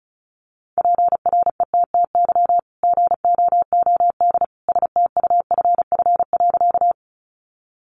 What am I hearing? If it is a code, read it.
Morse code, heard as a keyed tone, about 34 wpm, 700 Hz. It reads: PRETTY GOOD STUFF.
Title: Sentences from Top 1400 Words 35wpm